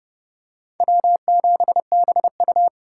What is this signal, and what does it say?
Morse code, heard as a keyed tone, about 30 wpm, 700 Hz. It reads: W7BU